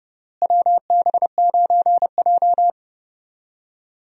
Morse code, heard as a keyed tone, about 32 wpm, 700 Hz. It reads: WB9J
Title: USA Call Signs - 4 Characters - Repeated 3x - 30wpm